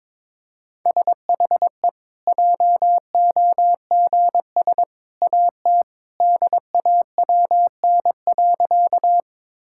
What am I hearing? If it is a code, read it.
Morse code, heard as a keyed tone, about 22 wpm, 700 Hz. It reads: SHE JOGS AT DAWN.
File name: practice / 22wpm / Easy Sentences with Hints - 22wpm